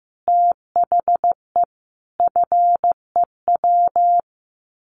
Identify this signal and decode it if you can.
Morse code, heard as a keyed tone, about 15 wpm, 700 Hz. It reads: THE FEW
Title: Sets of 5 Words 15wpm